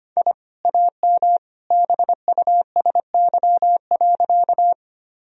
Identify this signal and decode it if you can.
Morse code, heard as a keyed tone, about 25 wpm, 700 Hz. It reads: I AM BUSY.
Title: Sentences from Top 1000 Words - Review Part 3 - 25wpm